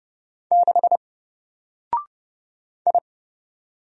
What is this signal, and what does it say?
Morse code, heard as a keyed tone, about 31 wpm, 700 Hz. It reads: 6 I
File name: Single-Letter-Number-30wpm